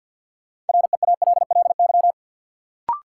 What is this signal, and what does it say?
Morse code, heard as a keyed tone, about 51 wpm, 700 Hz. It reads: REALLY
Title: Sets of 4 Words - New Word Drill - 50wpm